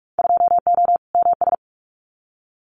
Morse code, heard as a keyed tone, about 44 wpm, 700 Hz. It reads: 2O MS
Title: Winter Field Day Exchanges 45wpm